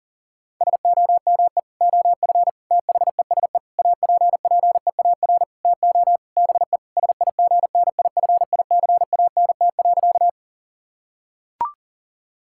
Morse code, heard as a keyed tone, about 40 wpm, 700 Hz. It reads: SOME OF THESE APPEAR TO BE SIGNIFICANT.
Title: Sentences from Top 1000 Words - Review Part 6 - 40wpm